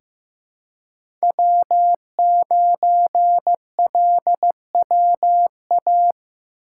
Morse code, heard as a keyed tone, about 15 wpm, 700 Hz. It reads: W9LWA
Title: USA Call Signs - 5 Characters - 2x Character Spacing - 15wpm